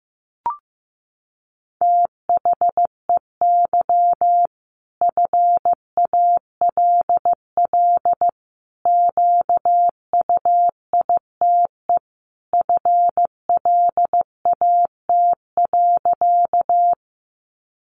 Morse code, heard as a keyed tone, about 14 wpm, 700 Hz. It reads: THEY FALL QUITE FLAT.
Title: Sentences from Top 1000 Words 15wpm